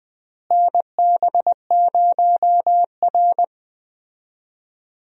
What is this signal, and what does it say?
Morse code, heard as a keyed tone, about 20 wpm, 700 Hz. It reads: NB0R
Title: USA Call Signs - 4 Characters - Repeated 2x - 20wpm